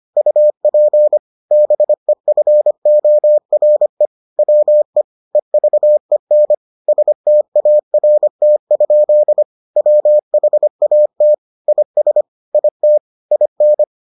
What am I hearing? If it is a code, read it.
Morse code, heard as a keyed tone, about 25 wpm, 600 Hz. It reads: UP BEFORE WE EVEN START? WHAT IS IT IN